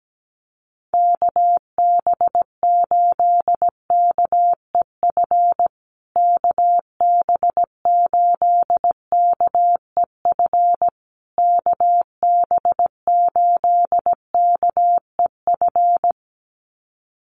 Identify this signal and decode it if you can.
Morse code, heard as a keyed tone, about 17 wpm, 700 Hz. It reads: KB8KEF KB8KEF KB8KEF
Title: USA Call Signs Repeated 3x 17wpm